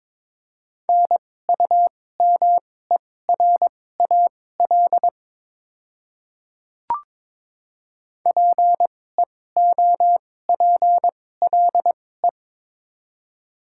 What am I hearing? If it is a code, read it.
Morse code, heard as a keyed tone, about 22 wpm, spaced as for 16 wpm, 700 Hz. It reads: NUMERAL PEOPLE
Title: 2x Character Spacing - Top 500 Words - 22wpm